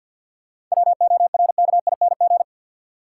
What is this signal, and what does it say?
Morse code, heard as a keyed tone, about 53 wpm, 700 Hz. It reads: WORKING
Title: Sentences from Top 700 Words - New Word Drill - 50wpm